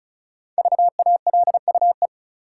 Morse code, heard as a keyed tone, about 36 wpm, 700 Hz. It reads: VALUE
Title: Five Letter Words - 35wpm